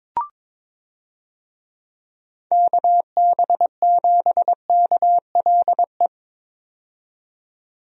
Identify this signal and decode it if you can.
Morse code, heard as a keyed tone, about 22 wpm, 700 Hz. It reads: KB7KLE
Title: USA Call Signs - 6 Characters - 22wpm